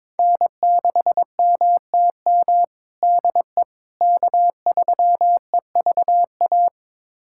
Morse code, heard as a keyed tone, about 22 wpm, 700 Hz. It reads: N6MTM DE K3EVA
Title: USA Callsign DE Callsign - 5 Characters - 22wpm